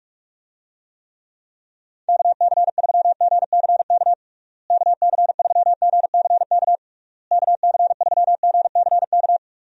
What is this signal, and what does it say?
Morse code, heard as a keyed tone, about 41 wpm, 700 Hz. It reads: KC3GCK KC3GCK KC3GCK
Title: USA Call Signs - 6 Characters - Repeated 3x - 45wpm